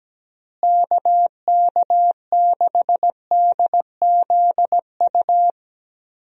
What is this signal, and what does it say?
Morse code, heard as a keyed tone, about 17 wpm, 700 Hz. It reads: KK6DZU
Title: USA Call Signs - 6 Characters - Repeated 3x - 17wpm